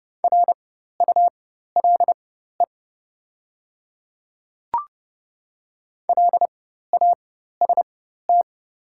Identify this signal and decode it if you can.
Morse code, heard as a keyed tone, about 30 wpm, spaced as for 14 wpm, 690 Hz. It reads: RULE LAST
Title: 4x Character Spacing - Top 500 Words - 30wpm